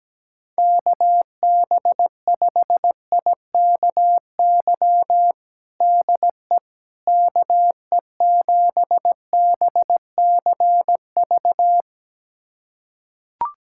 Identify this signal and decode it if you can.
Morse code, heard as a keyed tone, about 17 wpm, 700 Hz. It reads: KB5IKY DE KE7BCV
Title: USA Callsign DE Callsign - 6 Characters - 17wpm